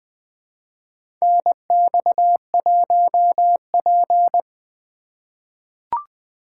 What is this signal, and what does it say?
Morse code, heard as a keyed tone, about 20 wpm, 700 Hz. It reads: NX1P
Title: USA Call Signs - 4 Characters - 3x Character Spacing - 20wpm